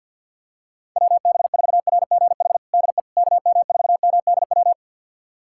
Morse code, heard as a keyed tone, about 51 wpm, 700 Hz. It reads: WB4RGH DE KM4MDW
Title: USA Callsign DE Callsign 50wpm